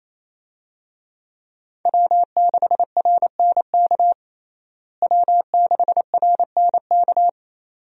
Morse code, heard as a keyed tone, about 28 wpm, 700 Hz. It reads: W6RNK W6RNK
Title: USA Call Signs - 5 Characters - Repeated 2x - 28wpm